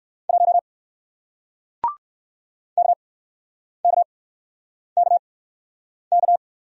Morse code, heard as a keyed tone, about 61 wpm, spaced as for 9 wpm, 700 Hz. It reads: 1 KKKK